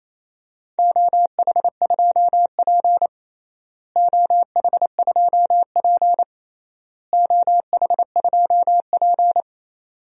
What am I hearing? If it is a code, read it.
Morse code, heard as a keyed tone, about 28 wpm, 700 Hz. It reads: OH2P OH2P OH2P
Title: DXCC-100 Call Signs - 4 Characters - Repeated 3x - 28wpm